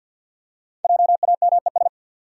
Morse code, heard as a keyed tone, about 53 wpm, 700 Hz. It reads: JAMES
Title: USA Common Names - Max 5 Letters - 50wpm